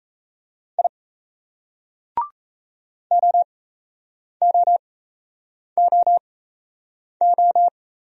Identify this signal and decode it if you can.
Morse code, heard as a keyed tone, about 35 wpm, spaced as for 8 wpm, 700 Hz. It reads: I OOOO
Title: Speed-Racing Single Letters 28wpm